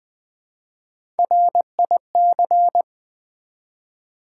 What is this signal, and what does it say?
Morse code, heard as a keyed tone, about 20 wpm, 700 Hz. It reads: RIC
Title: Build A Word - 1 Letter At A Time - 3 to 5-Letter Words - 20wpm